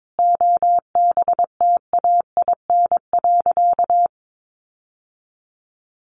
Morse code, heard as a keyed tone, about 22 wpm, 700 Hz. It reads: OBTAIN.
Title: Sentences from Top 1300 Words 22wpm